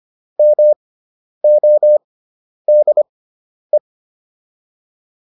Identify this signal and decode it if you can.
Morse code, heard as a keyed tone, about 25 wpm, spaced as for 10 wpm, 600 Hz. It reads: MODE